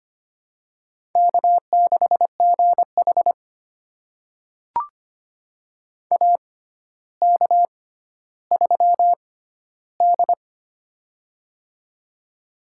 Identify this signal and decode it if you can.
Morse code, heard as a keyed tone, about 25 wpm, spaced as for 13 wpm, 700 Hz. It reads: K6GH AK3D
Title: USA Call Signs - 4 Characters - 6x Character Spacing - 25wpm